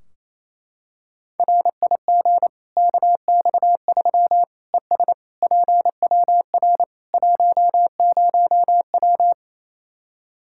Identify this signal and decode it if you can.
Morse code, heard as a keyed tone, about 28 wpm, 700 Hz. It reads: RIG KX3 ES PWR 10W
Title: Practice QSO - Exchange 2 - Info - 28wpm